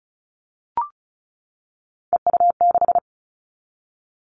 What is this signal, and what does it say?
Morse code, heard as a keyed tone, about 36 wpm, 700 Hz. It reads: EU6
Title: Callsign Prefixes for Top 50 DXCC - 35wpm